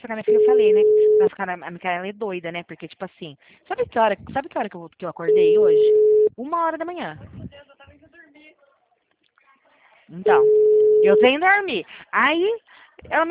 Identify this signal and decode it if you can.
Ringback tone (Europe, 425 Hz), with speech.